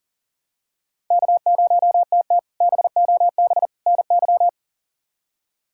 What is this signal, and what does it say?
Morse code, heard as a keyed tone, about 39 wpm, 700 Hz. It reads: K0TT BOB NY
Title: K1USN SST QSO Practice - 40wpm